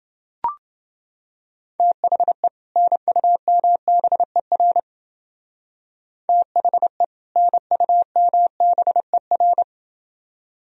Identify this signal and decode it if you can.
Morse code, heard as a keyed tone, about 29 wpm, 700 Hz. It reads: THE NUMBER THE NUMBER